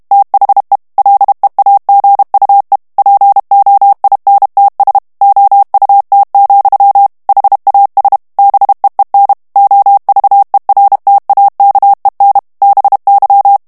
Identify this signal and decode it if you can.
Morse code, heard as a keyed tone, about 32 wpm, 800 Hz. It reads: THE LEAGUE POINTS OUT, HAS BEEN OVERTAKEN BY
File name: ANGLAIS texte clair 3 vitesse 30 mots minute